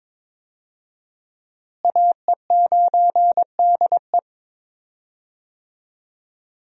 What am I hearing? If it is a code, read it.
Morse code, heard as a keyed tone, about 21 wpm, 700 Hz. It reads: AE9DE
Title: USA Call Signs - 5 Characters - Repeated 2x - 22wpm